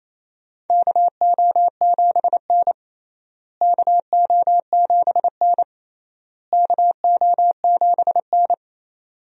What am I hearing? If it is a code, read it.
Morse code, heard as a keyed tone, about 28 wpm, 700 Hz. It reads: KO7N KO7N KO7N
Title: USA Call Signs - 4 Characters - Repeated 3x - 28wpm